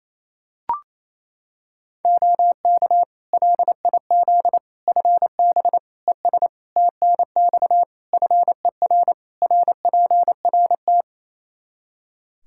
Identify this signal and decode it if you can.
Morse code, heard as a keyed tone, about 28 wpm, 700 Hz. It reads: OK LIZ FB ES TNX FER RPRT
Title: K1USN SST QSO Practice 28wpm